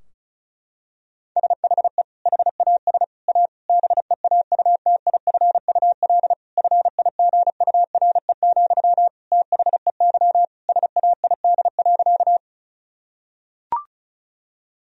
Morse code, heard as a keyed tone, about 34 wpm, 700 Hz. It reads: SHE HAS A BEAUTIFUL FIGURE, THEY SAID.
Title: Sentences from Top 1000 Words - Review Part 2 - 35wpm